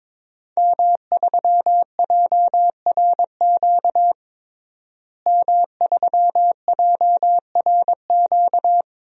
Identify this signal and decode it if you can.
Morse code, heard as a keyed tone, about 22 wpm, 700 Hz. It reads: M3JRQ M3JRQ
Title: DXCC-100 Call Signs - 5 Characters - Repeated 2x - 22wpm